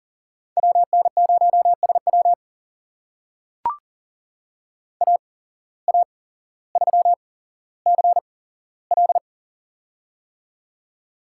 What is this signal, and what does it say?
Morse code, heard as a keyed tone, about 40 wpm, spaced as for 17 wpm, 700 Hz. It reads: WN0SW AA3CL
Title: USA Call Signs - 5 Characters - 8x Character Spacing - 40wpm